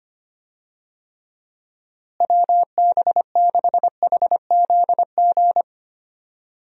Morse code, heard as a keyed tone, about 25 wpm, 700 Hz. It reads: WB6HZG
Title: USA Call Signs - 6 Characters - Repeated 3x - 25wpm